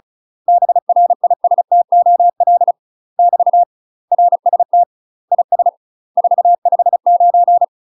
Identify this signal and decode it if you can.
Morse code, heard as a keyed tone, about 36 wpm, 700 Hz. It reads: BRISTOL <BT> RST IS 459
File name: UK MASTER QSOs TRAINING / 35WPM / UK MASTER QSOs 0012